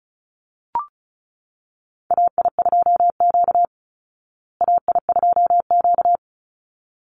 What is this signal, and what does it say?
Morse code, heard as a keyed tone, about 34 wpm, 700 Hz. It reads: AI2Q AI2Q
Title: Call Signs Repeated 3x - 35wpm